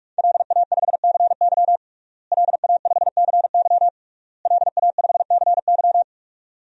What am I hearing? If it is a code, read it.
Morse code, heard as a keyed tone, about 46 wpm, 700 Hz. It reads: LA5CY LA5CY LA5CY
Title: DXCC-100 Call Signs - 5 Characters - Repeated 3x - 45wpm